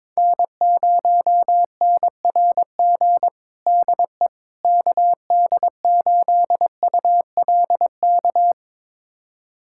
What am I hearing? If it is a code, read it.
Morse code, heard as a keyed tone, about 22 wpm, 700 Hz. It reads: N0NRG DE KD8ULK
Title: USA Callsign DE Callsign 22wpm